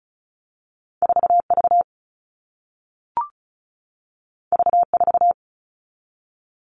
Morse code, heard as a keyed tone, about 36 wpm, 700 Hz. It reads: 4V V4
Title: All Cut Numbers - 2-Digit 35wpm